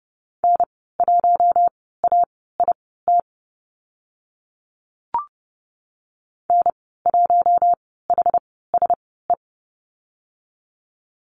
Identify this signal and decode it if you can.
Morse code, heard as a keyed tone, about 30 wpm, spaced as for 17 wpm, 700 Hz. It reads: N1AIT N1HSE
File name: Lesson 14g - Number 2 - Call Signs 30wpm